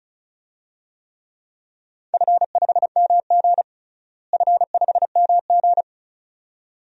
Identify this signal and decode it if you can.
Morse code, heard as a keyed tone, about 36 wpm, 700 Hz. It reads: F5MG F5MG
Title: DXCC-100 Call Signs - 4 Characters - Repeated 2x - 35wpm